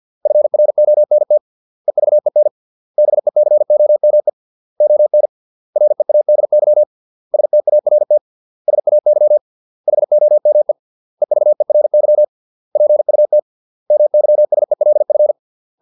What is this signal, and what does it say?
Morse code, heard as a keyed tone, about 50 wpm, 600 Hz. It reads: FRONT EVEN BECOME ON READY START SAY HOME EVERY PUT MYSELF